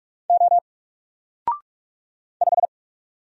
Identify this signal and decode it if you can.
Morse code, heard as a keyed tone, about 46 wpm, 700 Hz. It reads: O 5
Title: Single Letter-Number Alt 45wpm